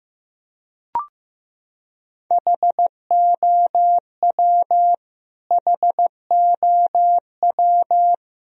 Morse code, heard as a keyed tone, about 15 wpm, 700 Hz. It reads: HOW HOW
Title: Top 100 Words Repeated 15wpm